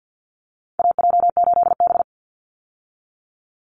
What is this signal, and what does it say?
Morse code, heard as a keyed tone, about 50 wpm, 700 Hz. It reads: AJ8B
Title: Call Signs Repeated 3x - 50wpm